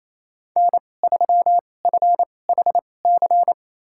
Morse code, heard as a keyed tone, about 28 wpm, spaced as for 20 wpm, 700 Hz. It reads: N3FHC
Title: USA Call Signs - 5 Characters - 2x Character Spacing - 28wpm